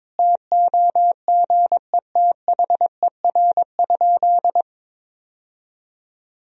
Morse code, heard as a keyed tone, about 22 wpm, 700 Hz. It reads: TOGETHER?
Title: Sentences from Top 1000 Words - Review Part 1 - 22wpm